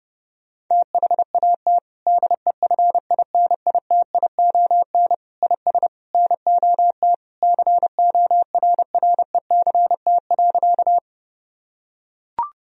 Morse code, heard as a keyed tone, about 30 wpm, 700 Hz. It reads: THAT DEFINITION IS NOT CORRECT.
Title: Sentences from Top 1000 Words - Review Part 4 - 30wpm